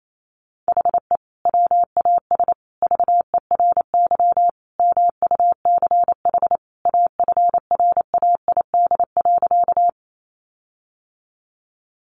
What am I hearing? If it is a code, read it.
Morse code, heard as a keyed tone, about 28 wpm, 700 Hz. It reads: HE WAS VERY MUCH AFRAID.
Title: Sentences from Top 1500 Words 28wpm